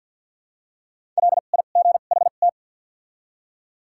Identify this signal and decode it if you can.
Morse code, heard as a keyed tone, about 53 wpm, spaced as for 37 wpm, 700 Hz. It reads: LIGHT
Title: 2x Character Spacing - Top 300 Words - 50wpm